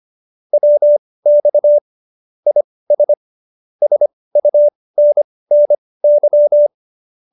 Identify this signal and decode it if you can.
Morse code, heard as a keyed tone, about 25 wpm, spaced as for 18 wpm, 590 Hz. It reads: WX IS SUNNY
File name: qso216b 18